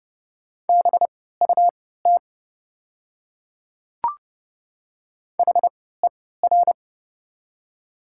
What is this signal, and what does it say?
Morse code, heard as a keyed tone, about 31 wpm, spaced as for 17 wpm, 690 Hz. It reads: BUT HER
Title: 3x Character Spacing - Top 100 Words - 30wpm